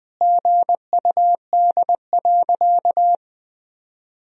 Morse code, heard as a keyed tone, about 21 wpm, 700 Hz. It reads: GUD.
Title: Sentences from Top 200 Words and Abbreviations 20wpm